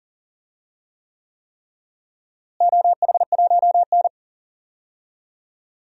Morse code, heard as a keyed tone, about 40 wpm, 700 Hz. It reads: OH1N
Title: DXCC-100 Call Signs - 4 Characters - Repeated 3x - 40wpm